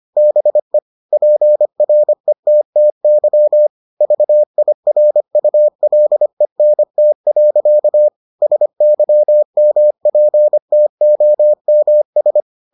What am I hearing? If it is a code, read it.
Morse code, heard as a keyed tone, about 25 wpm, 600 Hz. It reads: BE PRETTY VIRULENT. SYMPTOMS